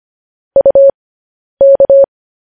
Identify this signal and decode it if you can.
Morse code, heard as a keyed tone, about 25 wpm, spaced as for 10 wpm, 550 Hz. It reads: UK
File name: podcast / News headlines-1x-25x10